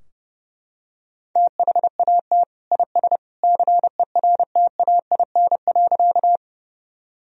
Morse code, heard as a keyed tone, about 32 wpm, 700 Hz. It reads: THAT IS CERTAIN.
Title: Sentences from Top 1000 Words - Review Part 2 - 30wpm